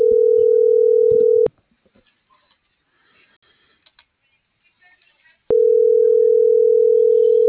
Ringback tone (North America), 440 + 480 Hz.